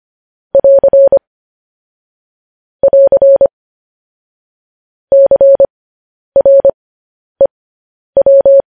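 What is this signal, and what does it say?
Morse code, heard as a keyed tone, about 25 wpm, spaced as for 10 wpm, 550 Hz. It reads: <AR> <AR> CREW
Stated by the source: Morse code practice of news headlines at 10 wpm.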